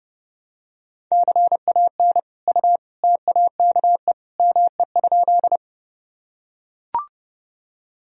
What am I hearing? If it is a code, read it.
Morse code, heard as a keyed tone, about 30 wpm, 700 Hz. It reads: CAN U TAKE ME?
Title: Sentences from Top 200 Words and Abbreviations 30wpm